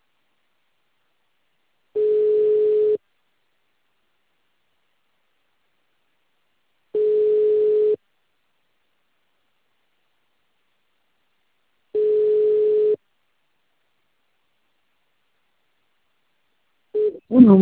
Ringback tone (Europe, 425 Hz).
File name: out- -2044- - - .85175